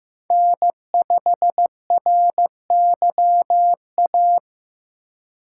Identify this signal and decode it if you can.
Morse code, heard as a keyed tone, about 15 wpm, 700 Hz. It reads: N5RYA
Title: USA Call Signs - 8x Character Spacing - 15wpm